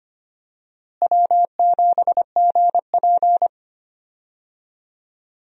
Morse code, heard as a keyed tone, about 25 wpm, 700 Hz. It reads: W7GP
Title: USA Call Signs - 4 Characters - Repeated 3x - 25wpm